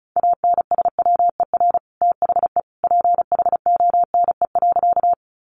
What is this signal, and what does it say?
Morse code, heard as a keyed tone, about 34 wpm, 700 Hz. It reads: ANSWER THE PHONE.
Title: Sentences from Top 1100 Words 35wpm